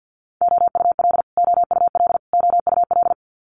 Morse code, heard as a keyed tone, about 50 wpm, 700 Hz. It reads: OUL OUL OUL
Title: Multi-letter Phonemes Repeated 3x 50wpm